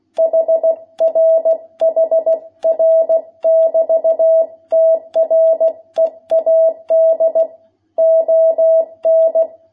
Morse code, heard as a keyed tone, about 16 wpm, 660 Hz. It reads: HR HR <BT> TREAD ON
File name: jh8cbh / 2021cw